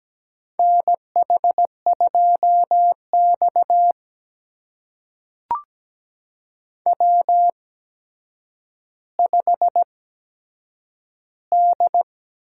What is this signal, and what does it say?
Morse code, heard as a keyed tone, about 17 wpm, spaced as for 8 wpm, 700 Hz. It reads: NH2X W5D
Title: USA Call Signs - 4 Characters - 8x Character Spacing - 17wpm